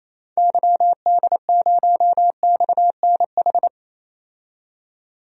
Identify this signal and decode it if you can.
Morse code, heard as a keyed tone, about 28 wpm, 700 Hz. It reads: YD0XNH
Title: DXCC-100 Call Signs - 6 Characters - 28wpm